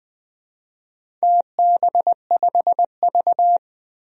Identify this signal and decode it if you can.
Morse code, heard as a keyed tone, about 20 wpm, 700 Hz. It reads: TB5V
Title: DXCC-100 Call Signs - 4 Characters - 20wpm